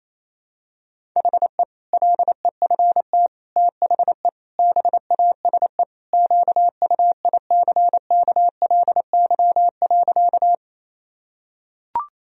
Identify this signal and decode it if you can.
Morse code, heard as a keyed tone, about 28 wpm, 700 Hz. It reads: HE LEFT THE BASE QUICKLY.
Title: Sentences from Top 1000 Words - Review Part 1 - 28wpm